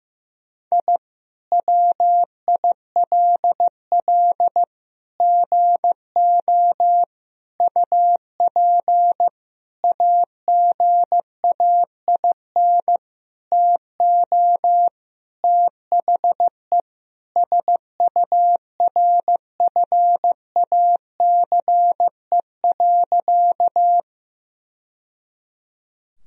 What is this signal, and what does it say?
Morse code, heard as a keyed tone, about 15 wpm, 700 Hz. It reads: I WILL GO UP AGAIN TO THE SURFACE.
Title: Sentences from Top 1000 Words - Review Part 3 - 15wpm